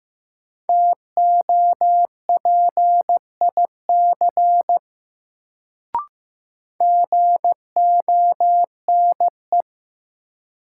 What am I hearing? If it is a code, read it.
Morse code, heard as a keyed tone, about 15 wpm, 700 Hz. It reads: TOPIC GONE
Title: Sentences from Top 1200 Words - New Word Drill - 15wpm